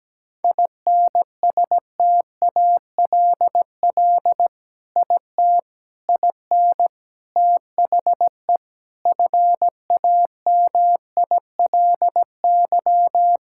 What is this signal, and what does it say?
Morse code, heard as a keyed tone, about 17 wpm, 700 Hz. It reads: INSTALL IT IN THE FAMILY
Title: Sentences from Top 1600 Words 17wpm